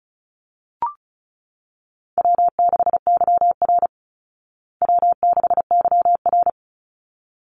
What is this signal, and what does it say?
Morse code, heard as a keyed tone, about 34 wpm, 700 Hz. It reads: W6YR W6YR
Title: Call Signs Repeated 3x - 35wpm